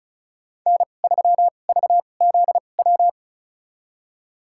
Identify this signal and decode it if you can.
Morse code, heard as a keyed tone, about 34 wpm, spaced as for 25 wpm, 700 Hz. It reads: N3VZW
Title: USA Call Signs - 5 Characters - 2x Character Spacing - 35wpm